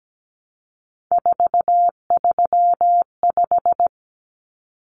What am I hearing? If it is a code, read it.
Morse code, heard as a keyed tone, about 17 wpm, 700 Hz. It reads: 435
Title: Common Cut Numbers - 3-Digit 17wpm